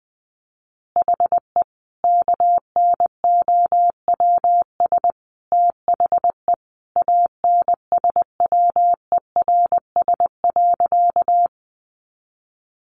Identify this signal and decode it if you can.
Morse code, heard as a keyed tone, about 20 wpm, 700 Hz. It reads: HE KNOWS THE ANSWERS.
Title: Additional Easy Phrases 20wpm